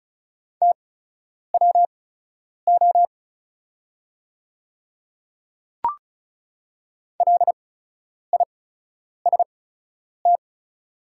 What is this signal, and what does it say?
Morse code, heard as a keyed tone, about 36 wpm, spaced as for 10 wpm, 690 Hz. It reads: TWO LIST
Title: 8x Character Spacing - Top 500 Words - 35wpm